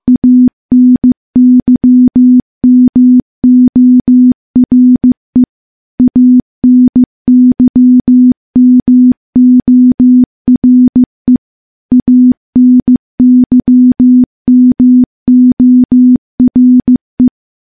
Morse code, heard as a keyed tone, about 16 wpm, 250 Hz. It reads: ANYMORE ANYMORE ANYMORE
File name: morse / morse-training-15WPM